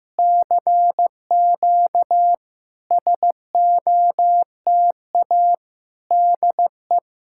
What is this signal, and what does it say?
Morse code, heard as a keyed tone, about 14 wpm, 700 Hz. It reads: CQ SOTA DE
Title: SOTA QSO Practice 15wpm